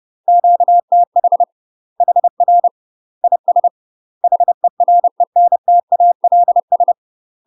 Morse code, heard as a keyed tone, about 30 wpm, 700 Hz. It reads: QTH HR IS HERENTALS
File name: UK MASTER QSOs TRAINING / 30WPM / UK MASTER QSOs 0040